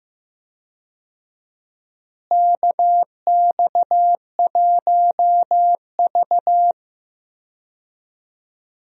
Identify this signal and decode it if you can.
Morse code, heard as a keyed tone, about 15 wpm, 700 Hz. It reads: KX1V
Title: USA Call Signs - 4 Characters - Repeated 3x - 15wpm